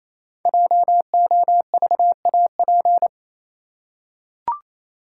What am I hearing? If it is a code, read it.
Morse code, heard as a keyed tone, about 27 wpm, 700 Hz. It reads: JOVAP
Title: Code Groups - 5 Letters - 28wpm